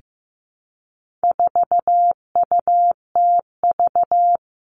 Morse code, heard as a keyed tone, about 15 wpm, 700 Hz. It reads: 4UTV
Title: All Cut Numbers - 4-Digit 15wpm